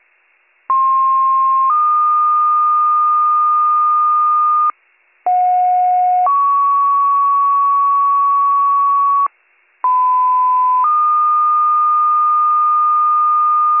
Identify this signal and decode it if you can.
Two-tone page, heard as a one-second tone, then a three-second tone, about 1060 then 1220 Hz, about 730 then 1120 Hz, about 1010 then 1220 Hz. It is heard from 0.7 to 13.8 s.